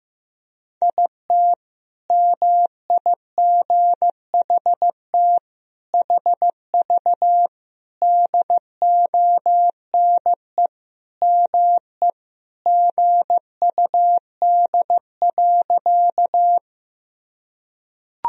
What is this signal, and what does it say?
Morse code, heard as a keyed tone, about 15 wpm, 700 Hz. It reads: IT MIGHT HV DONE ME GUD.
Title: Sentences from Top 500 Words and Abbreviations 15wpm